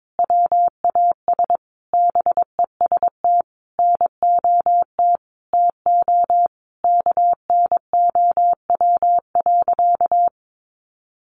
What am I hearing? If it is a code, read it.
Morse code, heard as a keyed tone, about 22 wpm, 700 Hz. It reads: WAS BEST NOT TO KNOW.
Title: Sentences from Top 600 Words 22wpm